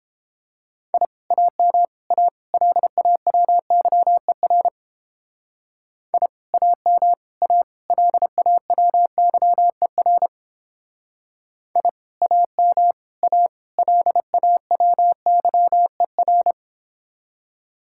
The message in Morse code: I AM A LAWYER I AM A LAWYER I AM A LAWYER

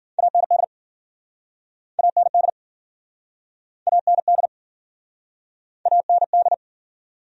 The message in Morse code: AND AND AND AND

